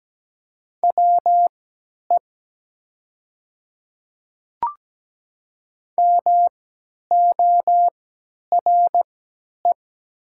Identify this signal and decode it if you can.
Morse code, heard as a keyed tone, about 17 wpm, spaced as for 10 wpm, 700 Hz. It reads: WE MORE
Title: 3x Character Spacing - Top 500 Words - 17wpm